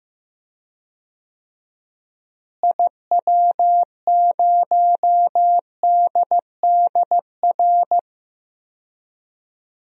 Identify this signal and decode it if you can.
Morse code, heard as a keyed tone, about 15 wpm, 700 Hz. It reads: IW0DDR
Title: DXCC-100 Call Signs - 6 Characters - Repeated 3x - 15wpm